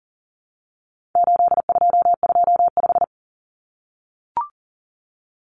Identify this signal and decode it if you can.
Morse code, heard as a keyed tone, about 40 wpm, 700 Hz. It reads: 8225
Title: Numbers - 4-Digit 40wpm